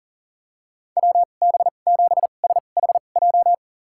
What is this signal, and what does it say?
Morse code, heard as a keyed tone, about 40 wpm, spaced as for 29 wpm, 700 Hz. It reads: WB7SHJ
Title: USA Call Signs - 6 Characters - 2x Character Spacing - 40wpm